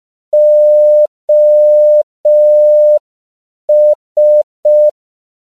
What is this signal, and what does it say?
Morse code, heard as a keyed tone, about 5 wpm, 600 Hz. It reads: OS
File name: CW / 5PPM